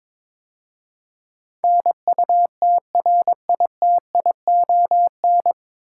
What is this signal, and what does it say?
Morse code, heard as a keyed tone, about 22 wpm, 700 Hz. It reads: NUTRITION
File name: 9 Letter Words - First 3 Letters Given - 22wpm